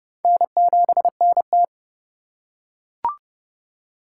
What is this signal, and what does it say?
Morse code, heard as a keyed tone, about 30 wpm, 700 Hz. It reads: N7NT
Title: K1USN SST QSO Practice - 30wpm